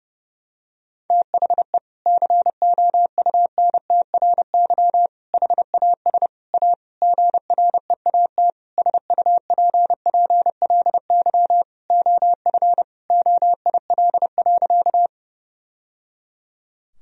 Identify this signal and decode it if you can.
Morse code, heard as a keyed tone, about 30 wpm, 700 Hz. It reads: THE COUNTRY HAS A GREAT SUPPLY OF OIL.
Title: Sentences from Top 1000 Words - Review Part 4 - 30wpm